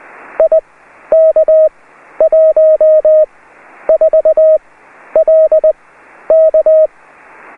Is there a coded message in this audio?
Yes: Morse code from 0.4 to 6.9 s, heard as a keyed tone, about 20 wpm, spaced as for 11 wpm, 630 Hz. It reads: IK14LK